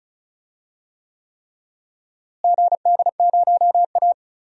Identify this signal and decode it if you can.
Morse code, heard as a keyed tone, about 36 wpm, 700 Hz. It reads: GD0A
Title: DXCC-100 Call Signs - 4 Characters - Repeated 3x - 35wpm